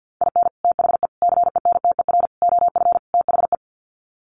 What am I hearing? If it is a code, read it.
Morse code, heard as a keyed tone, about 50 wpm, 700 Hz. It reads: IN THE CENTER OF THE
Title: Sets of 5 Words 50wpm